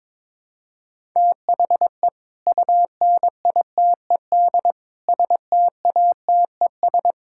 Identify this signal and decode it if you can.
Morse code, heard as a keyed tone, about 22 wpm, 700 Hz. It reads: THE UNITED STATES
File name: practice / 22wpm / Sentences with 4-Word Phrases - 22wpm